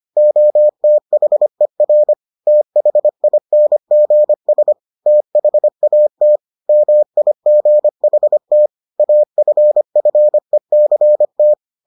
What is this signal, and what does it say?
Morse code, heard as a keyed tone, about 25 wpm, 600 Hz. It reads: OTHER THINGS THAT MIGHT AFFECT